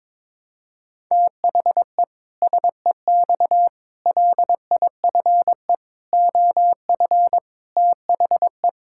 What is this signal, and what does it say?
Morse code, heard as a keyed tone, about 22 wpm, 700 Hz. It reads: THE SEX LIFE OF THE
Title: Pangram Challenge v2 - 22wpm